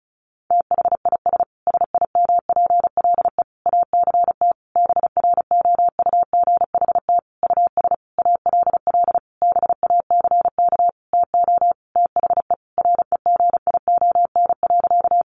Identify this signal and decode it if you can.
Morse code, heard as a keyed tone, about 34 wpm, 700 Hz. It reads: THIS SIMPLE ACT BROUGHT US ALL BACK TO THE REGION.
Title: Sentences from Top 1500 Words 35wpm